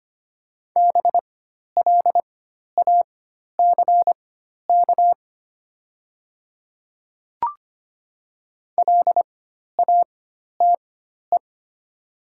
Morse code, heard as a keyed tone, about 25 wpm, spaced as for 12 wpm, 700 Hz. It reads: BLACK LATE
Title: 4x Character Spacing - Top 500 Words - 25wpm